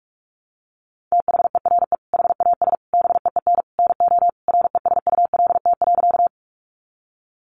Morse code, heard as a keyed tone, about 44 wpm, 700 Hz. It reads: THERE HAS BEEN NO RESULT.
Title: Sentences from Top 700 Words 45wpm